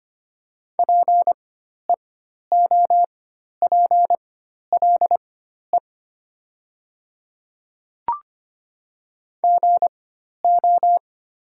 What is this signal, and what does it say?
Morse code, heard as a keyed tone, about 25 wpm, spaced as for 12 wpm, 700 Hz. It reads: PEOPLE GO
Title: 4x Character Spacing - Top 500 Words - 25wpm